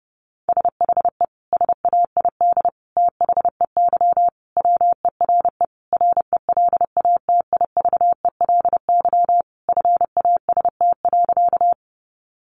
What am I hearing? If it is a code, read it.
Morse code, heard as a keyed tone, about 30 wpm, 700 Hz. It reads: SHE SAID THEY WERE RELATIVELY FAST.
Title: Sentences from Top 1100 Words 30wpm